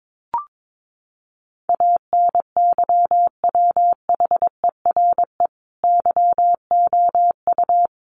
Morse code, heard as a keyed tone, about 22 wpm, 700 Hz. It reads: ANYWHERE YOU
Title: Sentences from Top 1400 Words 22wpm